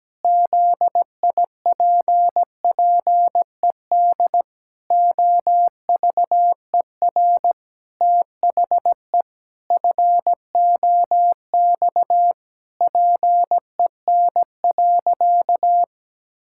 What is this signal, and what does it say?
Morse code, heard as a keyed tone, about 17 wpm, 700 Hz. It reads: ZIPPED OVER THE FOX PEN.
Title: Pangram Challenge v2 - 17wpm